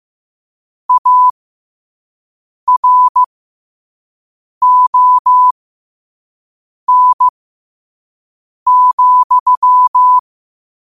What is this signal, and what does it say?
Morse code, heard as a keyed tone, about 15 wpm, spaced as for 5 wpm, 1000 Hz. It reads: ARON,
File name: qotdfiles / 5wpm